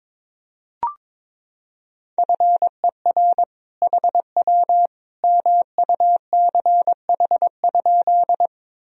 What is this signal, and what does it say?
Morse code, heard as a keyed tone, about 22 wpm, 700 Hz. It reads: FER HW MUCH?
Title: Sentences from Top 200 Words and Abbreviations 22wpm